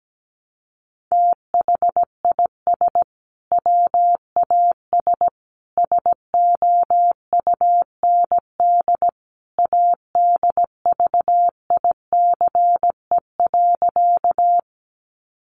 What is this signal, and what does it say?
Morse code, heard as a keyed tone, about 17 wpm, 700 Hz. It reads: THIS WAS SOUND ADVICE.
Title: Sentences from Top 1100 Words 17wpm